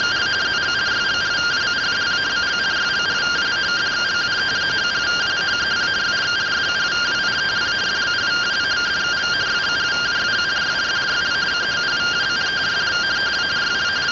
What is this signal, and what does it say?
Two-tone teleprinter signal, about 170 Hz shift.
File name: findings / CT AXIAL -377.gen-18 rtty rx